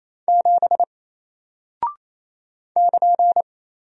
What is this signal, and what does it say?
Morse code, heard as a keyed tone, about 27 wpm, 700 Hz. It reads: 7 <KN>
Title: Single Character (Letter:Number:Punctuation:Prosign) Alt 28wpm